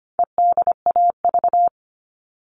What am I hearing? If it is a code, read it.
Morse code, heard as a keyed tone, about 25 wpm, 700 Hz. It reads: EDAV
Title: All Cut Numbers - 4-Digit 25wpm